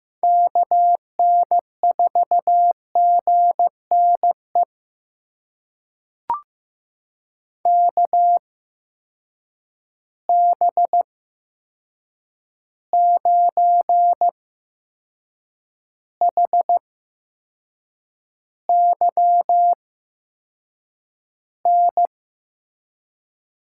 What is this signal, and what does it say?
Morse code, heard as a keyed tone, about 15 wpm, spaced as for 6 wpm, 700 Hz. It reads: KN4GNE KB9HYN
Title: USA Call Signs - 6 Characters - 8x Character Spacing - 15wpm